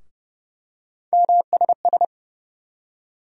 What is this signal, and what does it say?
Morse code, heard as a keyed tone, about 30 wpm, 700 Hz. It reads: MSS
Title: USA Call Sign Postfixes - 3 Letters 30wpm